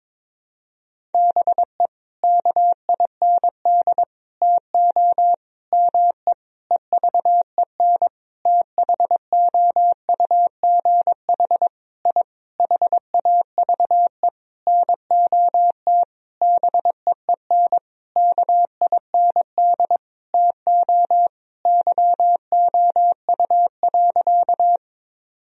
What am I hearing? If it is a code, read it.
Morse code, heard as a keyed tone, about 22 wpm, 700 Hz. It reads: BE KIND TO ME EVEN THOUGH I HAVE NOT BEEN KIND TO YOU.
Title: Sentences from Top 1000 Words - Review Part 2 - 22wpm